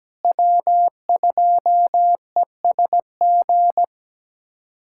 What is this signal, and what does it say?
Morse code, heard as a keyed tone, about 17 wpm, 700 Hz. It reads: W2ESG
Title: USA Callsign DE Callsign - 5 Characters - 17wpm